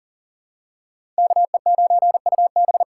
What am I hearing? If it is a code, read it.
Morse code, heard as a keyed tone, about 40 wpm, 700 Hz. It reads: KE9UB
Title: USA Call Signs - 5 Characters - 4x Character Spacing - 40wpm